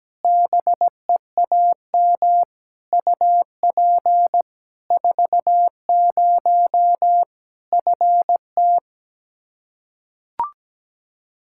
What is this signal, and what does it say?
Morse code, heard as a keyed tone, about 17 wpm, 700 Hz. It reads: BEAM UP 40 FT
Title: Practice QSO - Exchange 2 - Info - 17wpm